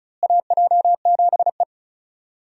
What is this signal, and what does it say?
Morse code, heard as a keyed tone, about 34 wpm, 700 Hz. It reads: AJ7E
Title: USA Call Signs - 4 Characters - Repeated 2x - 35wpm